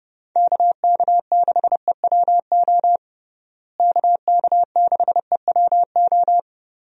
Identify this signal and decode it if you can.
Morse code, heard as a keyed tone, about 30 wpm, 700 Hz. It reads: KK6EWO KK6EWO
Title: USA Call Signs - 6 Characters - Repeated 2x - 30wpm